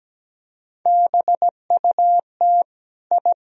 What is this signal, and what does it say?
Morse code, heard as a keyed tone, about 17 wpm, 700 Hz. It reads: BUT I
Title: Sentences from Top 1000 Words - Review Part 3 - 17wpm